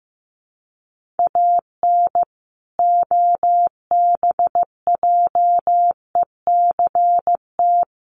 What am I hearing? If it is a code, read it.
Morse code, heard as a keyed tone, about 15 wpm, 700 Hz. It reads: AN OBJECT
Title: Sentences from Top 1400 Words 15wpm